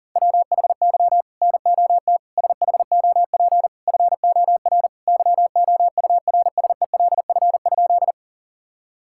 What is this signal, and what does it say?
Morse code, heard as a keyed tone, about 40 wpm, 700 Hz. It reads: WHY NOT SHOP FOR YOURSELF?
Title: Sentences from Top 1600 Words 40wpm